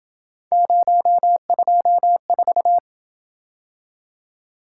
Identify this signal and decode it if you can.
Morse code, heard as a keyed tone, about 28 wpm, 700 Hz. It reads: 024